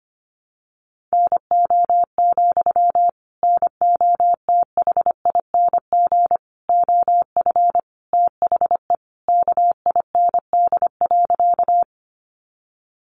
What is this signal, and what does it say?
Morse code, heard as a keyed tone, about 25 wpm, 700 Hz. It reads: NO, NOTHING OF THE KIND.
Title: Sentences from Top 500 Words 25wpm